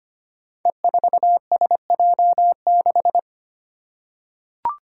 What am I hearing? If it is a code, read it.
Morse code, heard as a keyed tone, about 25 wpm, 700 Hz. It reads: E4SJ6
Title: Five Challenging Letter-Number Combinations 25wpm